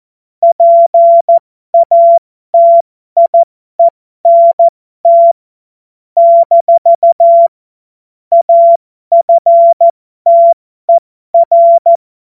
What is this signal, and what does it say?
Morse code, heard as a keyed tone, about 14 wpm, spaced as for 12 wpm, 680 Hz. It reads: PATIENT - AFTER
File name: Morsekurs / Witze